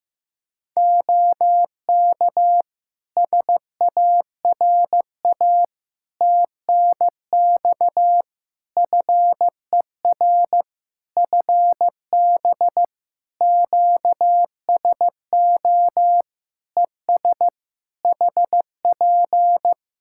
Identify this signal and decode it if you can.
Morse code, heard as a keyed tone, about 15 wpm, 700 Hz. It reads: OK SARA TNX FER FB QSO ES HP
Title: Practice QSO - Exchange 4 - Ending - 15wpm